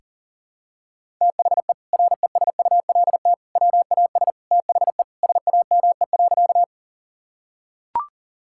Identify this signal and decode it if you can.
Morse code, heard as a keyed tone, about 37 wpm, 700 Hz. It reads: THE RESULT WAS THE SAME.
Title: Sentences from Top 1000 Words - Review Part 5 - 40wpm